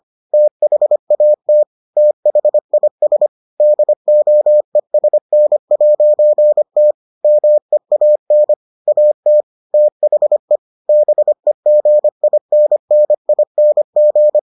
Morse code, heard as a keyed tone, about 25 wpm, 600 Hz. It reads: THAT THIS DOESN'T MEAN AT THE BEGINNING